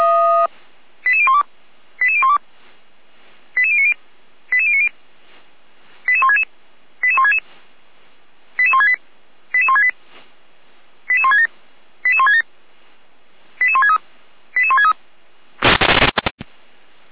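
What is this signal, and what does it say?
Five-tone selective call (ZVEI), heard as quick runs of five tones, tones of about 70 ms. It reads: 88012 88012 88090 88090 88166 88166 88168 88168 88167 88167 88163 88163
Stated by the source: ZVEI five-tone selective call 88208 received via SDR.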